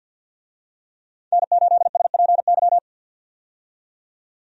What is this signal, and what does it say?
Morse code, heard as a keyed tone, about 50 wpm, 700 Hz. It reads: N8SPY
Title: USA Call Signs - 5 Characters - 50wpm